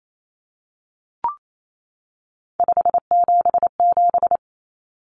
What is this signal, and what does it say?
Morse code, heard as a keyed tone, about 28 wpm, 700 Hz. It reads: 577
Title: Common Cut Numbers - 3-Digit 28wpm